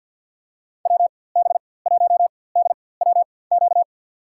Morse code, heard as a keyed tone, about 50 wpm, spaced as for 23 wpm, 700 Hz. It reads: WB1DWQ
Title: USA Call Signs - 6 Characters - 4x Character Spacing - 50wpm